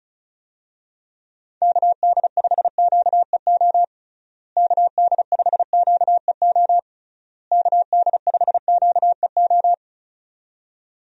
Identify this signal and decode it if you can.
Morse code, heard as a keyed tone, about 36 wpm, 700 Hz. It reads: KD5QEO KD5QEO KD5QEO
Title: USA Call Signs - 6 Characters - Repeated 3x - 35wpm